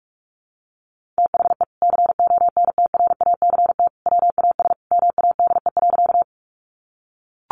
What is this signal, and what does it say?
Morse code, heard as a keyed tone, about 44 wpm, 700 Hz. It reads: THE CONTRACT WAS MADE.
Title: Sentences from Top 1400 Words 45wpm